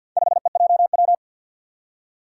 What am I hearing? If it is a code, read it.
Morse code, heard as a keyed tone, about 47 wpm, 700 Hz. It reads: 5EJW
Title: Four Challenging Letter-Number Combinations 50wpm